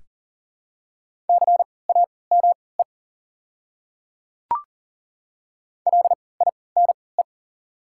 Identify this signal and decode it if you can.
Morse code, heard as a keyed tone, about 40 wpm, spaced as for 23 wpm, 690 Hz. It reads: CAME LINE
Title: 3x Character Spacing - Top 300 Words - 40wpm